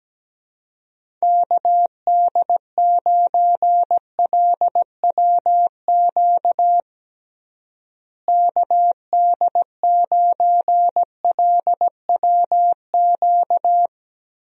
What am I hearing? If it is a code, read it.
Morse code, heard as a keyed tone, about 17 wpm, 700 Hz. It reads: KD9LWQ KD9LWQ
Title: USA Call Signs - 6 Characters - Repeated 3x - 17wpm